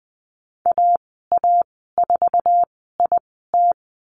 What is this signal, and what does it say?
Morse code, heard as a keyed tone, about 19 wpm, spaced as for 14 wpm, 700 Hz. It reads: AA4IT
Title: 055 Lesson 10g - Letter r - Call Signs 20wpm